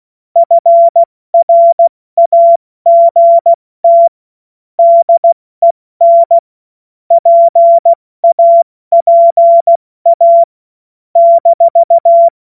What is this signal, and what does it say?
Morse code, heard as a keyed tone, about 16 wpm, 680 Hz. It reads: FRAGT DEN PAPA -
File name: Morsekurs / Witze / Witz 14. Februar 2023 70 F80